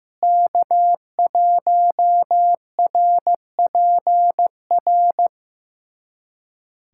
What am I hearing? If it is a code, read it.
Morse code, heard as a keyed tone, about 15 wpm, 700 Hz. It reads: K1RPR
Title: USA Callsign DE Callsign 15wpm